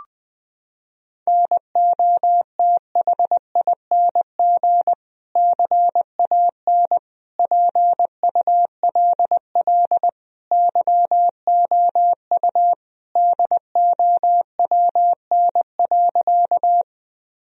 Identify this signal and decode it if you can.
Morse code, heard as a keyed tone, about 20 wpm, 700 Hz. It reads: NOTHING CAN PULL YOU DOWN.
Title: Sentences from Top 1000 Words - Review Part 2 - 20wpm